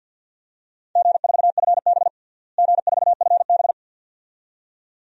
Morse code, heard as a keyed tone, about 52 wpm, 700 Hz. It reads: G4FB G4FB
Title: DXCC-100 Call Signs - 4 Characters - Repeated 2x - 50wpm